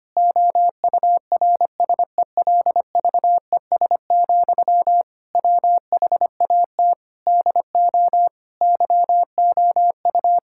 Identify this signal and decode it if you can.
Morse code, heard as a keyed tone, about 25 wpm, 700 Hz. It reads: OURSELVES, WHAT DO YOU
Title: Sentences from Top 1600 Words 25wpm